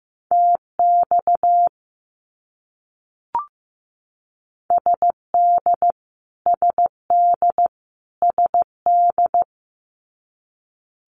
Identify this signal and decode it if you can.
Morse code, heard as a keyed tone, about 15 wpm, 700 Hz. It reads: TX SD SD SD
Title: US States Repeated 3x 15wpm